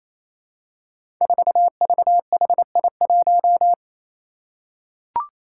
Morse code, heard as a keyed tone, about 28 wpm, 700 Hz. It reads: 4VHI1
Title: Five Challenging Letter-Number Combinations 28wpm